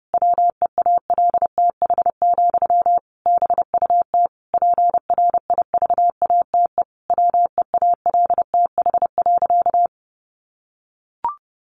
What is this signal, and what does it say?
Morse code, heard as a keyed tone, about 30 wpm, 700 Hz. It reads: WEALTH, BUT PRIVATE WEALTH.
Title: Sentences from Top 1200 Words 30wpm